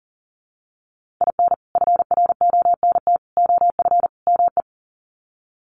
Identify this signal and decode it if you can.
Morse code, heard as a keyed tone, about 40 wpm, 700 Hz. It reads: IN FRONT OF ME
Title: Sets of 4 Words 40wpm